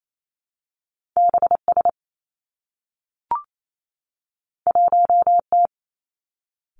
Morse code, heard as a keyed tone, about 28 wpm, 700 Hz. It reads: B S 1 T
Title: Two Challenging Letter-Number Combinations 28wpm